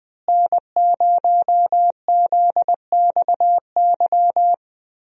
Morse code, heard as a keyed tone, about 20 wpm, 700 Hz. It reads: N0ZXY
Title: USA Call Signs - 5 Characters - Repeated 3x - 20wpm